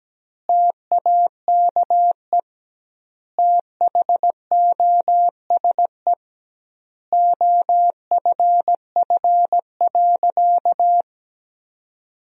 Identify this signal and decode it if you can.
Morse code, heard as a keyed tone, about 17 wpm, 700 Hz. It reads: TAKE THOSE OFF.
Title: 2x Word-Spacing - Sentences from Top 500 Words - 17wpm